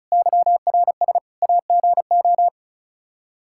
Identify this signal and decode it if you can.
Morse code, heard as a keyed tone, about 36 wpm, 700 Hz. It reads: YRS AGO
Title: Practice QSO - Exchange 3 - More Info - 35wpm